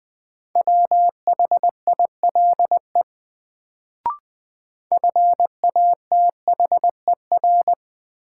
Morse code, heard as a keyed tone, about 21 wpm, 700 Hz. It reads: WHILE FATHER
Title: Sentences from Top 300 Words - New Word Drill - 20wpm